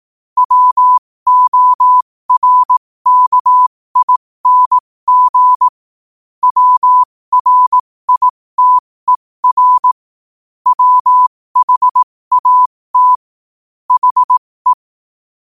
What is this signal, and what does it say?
Morse code, heard as a keyed tone, about 18 wpm, spaced as for 15 wpm, 1000 Hz. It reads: WORKING WRITER WHAT HE
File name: qotdfiles / 15wpm